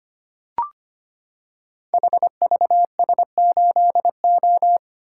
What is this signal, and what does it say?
Morse code, heard as a keyed tone, about 25 wpm, 700 Hz. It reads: HVS8O
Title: Code Groups - 5 Letter:Numbers - 25wpm